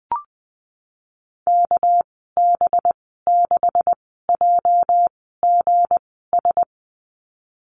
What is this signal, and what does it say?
Morse code, heard as a keyed tone, about 20 wpm, spaced as for 15 wpm, 700 Hz. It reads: KB6JGS
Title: 213 Lesson 34g - Number 8 - Call Signs 20wpm